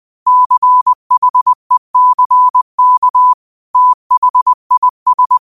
Morse code, heard as a keyed tone, about 20 wpm, 1000 Hz. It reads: CHECK THIS
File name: qotdfiles / 20wpm